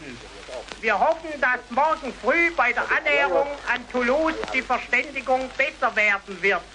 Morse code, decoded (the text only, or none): none